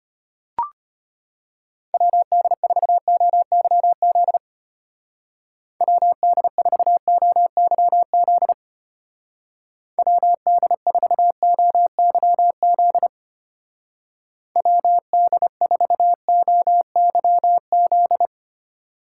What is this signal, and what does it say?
Morse code, heard as a keyed tone, about 31 wpm, 700 Hz. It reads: WD4OYZ WD4OYZ WD4OYZ WD4OYZ
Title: Speed Racing - USA 2x3 Call Signs 25wpm